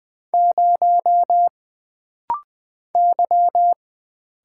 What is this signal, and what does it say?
Morse code, heard as a keyed tone, about 20 wpm, 700 Hz. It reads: 0 Y
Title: Single Letter-Number Alt - Rapid-Fire 20wpm